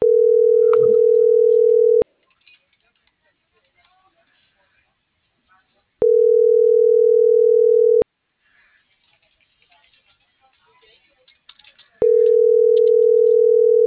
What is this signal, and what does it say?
Ringback tone (North America), 440 + 480 Hz.